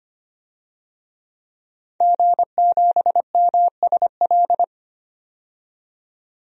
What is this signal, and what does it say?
Morse code, heard as a keyed tone, about 25 wpm, 700 Hz. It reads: G7MSL
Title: DXCC-100 Call Signs - 5 Characters - 25wpm